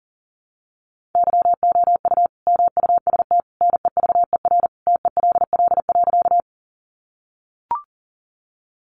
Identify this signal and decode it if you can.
Morse code, heard as a keyed tone, about 40 wpm, 700 Hz. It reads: YOU MUST NEVER TELL.
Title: Sentences from Top 300 Words 40wpm